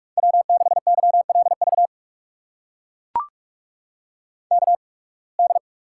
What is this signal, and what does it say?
Morse code, heard as a keyed tone, about 44 wpm, spaced as for 29 wpm, 700 Hz. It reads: W6YLV KD
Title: USA Call Signs - 8x Character Spacing - 45wpm